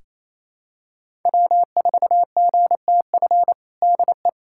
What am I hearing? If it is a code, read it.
Morse code, heard as a keyed tone, about 28 wpm, 700 Hz. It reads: W4GTF DE
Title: USA Callsign DE Callsign 28wpm